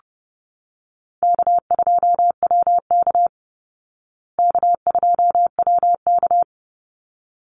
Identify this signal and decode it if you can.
Morse code, heard as a keyed tone, about 30 wpm, 700 Hz. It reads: K2WK K2WK
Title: Call Signs Repeated 3x - 30wpm